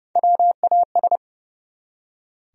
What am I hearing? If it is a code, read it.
Morse code, heard as a keyed tone, about 30 wpm, 700 Hz. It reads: WAS
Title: Top 100 Words 30wpm